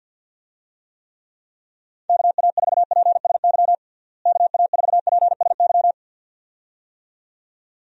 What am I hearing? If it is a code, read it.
Morse code, heard as a keyed tone, about 49 wpm, 700 Hz. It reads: KA4PSY KA4PSY
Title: USA Call Signs - 6 Characters - Repeated 2x - 50wpm